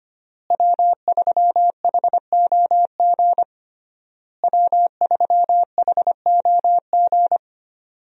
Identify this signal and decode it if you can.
Morse code, heard as a keyed tone, about 25 wpm, 700 Hz. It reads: W3HOG W3HOG
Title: USA Call Signs - 5 Characters - Repeated 3x - 25wpm